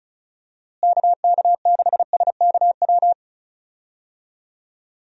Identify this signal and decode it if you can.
Morse code, heard as a keyed tone, about 36 wpm, 700 Hz. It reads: KK6SKW
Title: USA Call Signs - 6 Characters - Repeated 3x - 35wpm